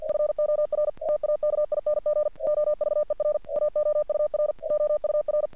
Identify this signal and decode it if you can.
Morse code, heard as a keyed tone, about 48 wpm, 600 Hz. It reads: FOR TAKING OVER AOUR OUR
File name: archived / CRSnet